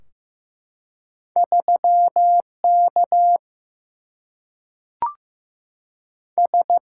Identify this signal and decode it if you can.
Morse code, heard as a keyed tone, about 15 wpm, 700 Hz. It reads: 3 K S
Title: Single-Letter Single-Number 15wpm